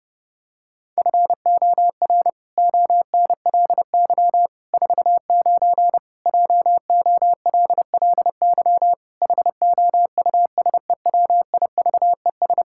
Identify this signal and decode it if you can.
Morse code, heard as a keyed tone, about 28 wpm, 700 Hz. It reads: FOR ONLY 49 JOLLY HOUSEWIVES
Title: Pangram Challenge v2 - 30wpm